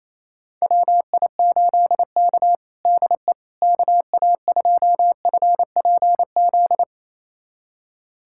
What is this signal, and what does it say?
Morse code, heard as a keyed tone, about 28 wpm, 700 Hz. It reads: WI8K DE KA2FPZ
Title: USA Callsign DE Callsign 28wpm